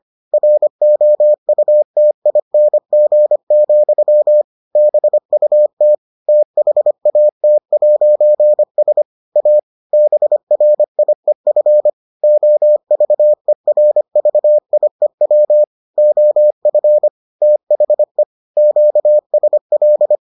Morse code, heard as a keyed tone, about 25 wpm, 600 Hz. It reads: ROUTING, BUT THAT'S A BRIEF OVERVIEW OF THE QSL